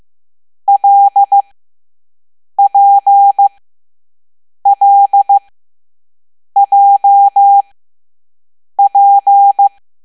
Morse code, heard as a keyed tone, about 15 wpm, spaced as for 6 wpm, 800 Hz. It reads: LPLJP